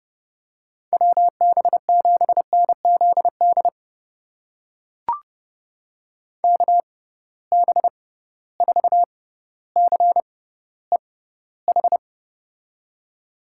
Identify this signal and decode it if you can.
Morse code, heard as a keyed tone, about 30 wpm, spaced as for 15 wpm, 700 Hz. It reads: WB7NZD KB4CEH
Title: USA Call Signs - 6 Characters - 6x Character Spacing - 30wpm